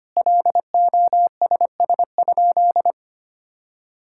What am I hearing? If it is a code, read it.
Morse code, heard as a keyed tone, about 25 wpm, 700 Hz. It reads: LOSS?
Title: Sentences from Top 1600 Words 25wpm